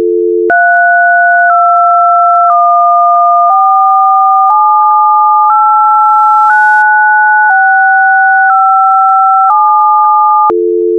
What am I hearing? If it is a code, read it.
DTMF, heard as telephone touch tones, tones of about 1010 ms. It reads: 3214789657